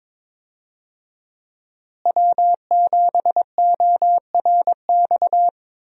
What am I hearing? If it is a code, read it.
Morse code, heard as a keyed tone, about 22 wpm, 700 Hz. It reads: W7ORX
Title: USA Call Signs - 5 Characters - Repeated 2x - 22wpm